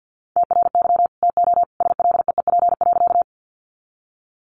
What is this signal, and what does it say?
Morse code, heard as a keyed tone, about 50 wpm, 700 Hz. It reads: TRY TO SLEEP.
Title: Sentences from Top 500 Words 50wpm